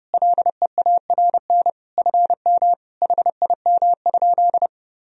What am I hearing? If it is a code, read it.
Morse code, heard as a keyed tone, about 32 wpm, 700 Hz. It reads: LEARN FM HIM?
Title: Sentences from Top 300 Words and Abbreviations 30wpm